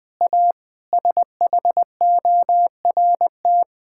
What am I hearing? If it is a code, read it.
Morse code, heard as a keyed tone, about 21 wpm, 700 Hz. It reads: A SHORT
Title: Sentences with 7-Word Phrases - 20wpm